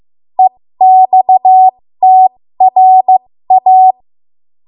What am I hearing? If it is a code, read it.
Morse code, heard as a keyed tone, about 15 wpm, spaced as for 13 wpm, 750 Hz. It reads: EXTRA
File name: Morse / 13WPM